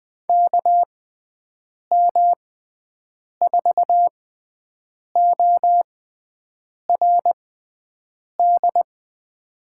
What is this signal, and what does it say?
Morse code, heard as a keyed tone, about 20 wpm, spaced as for 7 wpm, 700 Hz. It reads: KM4ORD
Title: USA Call Signs - 6 Characters - 6x Character Spacing - 20wpm